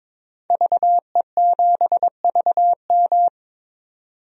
Morse code, heard as a keyed tone, about 22 wpm, 700 Hz. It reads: VE7VM
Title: DXCC-100 Call Signs - 5 Characters - 22wpm